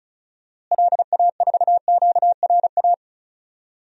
Morse code, heard as a keyed tone, about 36 wpm, 700 Hz. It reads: LA4QRA
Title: DXCC-100 Call Signs - 6 Characters - Repeated 3x - 35wpm